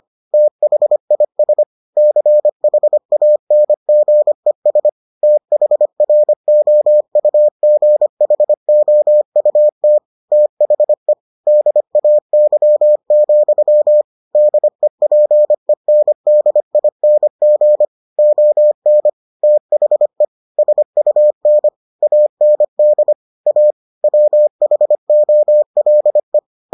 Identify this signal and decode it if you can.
Morse code, heard as a keyed tone, about 25 wpm, 600 Hz. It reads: THIS CHANGES THROUGHOUT THE DAY, DEPENDING ON THE SUN AND A WHOLE